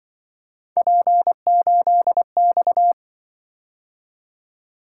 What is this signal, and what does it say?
Morse code, heard as a keyed tone, about 25 wpm, 700 Hz. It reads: P8X
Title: Speed-Racing - Three Letter-Number Combinations - 20wpm